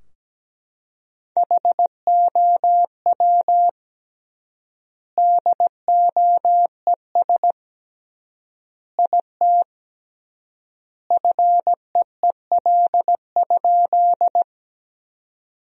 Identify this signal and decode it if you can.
Morse code, heard as a keyed tone, about 17 wpm, 700 Hz. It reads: HOW DOES IT FEEL?
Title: 3x Word-Spacing - Sentences from Top 400 Words - 17wpm